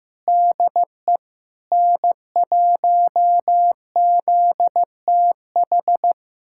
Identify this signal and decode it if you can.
Morse code, heard as a keyed tone, about 15 wpm, 700 Hz. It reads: DE N1ZTH
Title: USA Callsign DE Callsign - 5 Characters - 15wpm